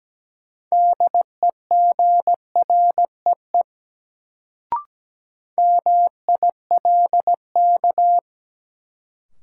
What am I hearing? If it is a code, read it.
Morse code, heard as a keyed tone, about 17 wpm, 700 Hz. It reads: DEGREE MILK
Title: Sentences from Top 1500 Words - New Word Drill - 17wpm